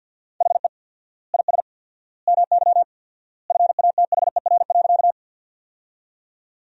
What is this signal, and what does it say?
Morse code, heard as a keyed tone, about 46 wpm, 700 Hz. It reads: HE IS MY FATHER.
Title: 4x Word-Spacing - Sentences from Top 300 Words - 50wpm